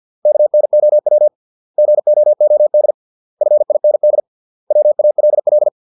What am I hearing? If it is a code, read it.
Morse code, heard as a keyed tone, about 50 wpm, 600 Hz. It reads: KNOW GOOD FIND WALL